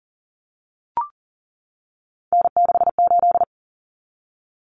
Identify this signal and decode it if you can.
Morse code, heard as a keyed tone, about 40 wpm, 700 Hz. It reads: N68
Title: Common Cut Numbers - 3-Digit 40wpm